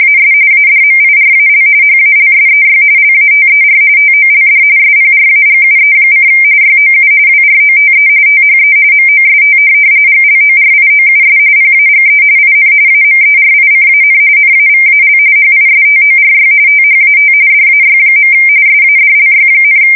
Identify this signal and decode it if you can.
RTTY, about 45.45 baud. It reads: AS ADMIRAL HALSEY LATER TOLD SPRAGUE: YOU WROTE THE MOST GLORIOUS PAGE IN AMERICAN NAVAL HISTORY THAT DAY. SADLY,
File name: fsk170 / USS-New-Jersey